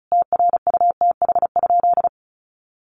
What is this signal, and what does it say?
Morse code, heard as a keyed tone, about 34 wpm, 700 Hz. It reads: TRUTH?
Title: Sentences from Top 800 Words 35wpm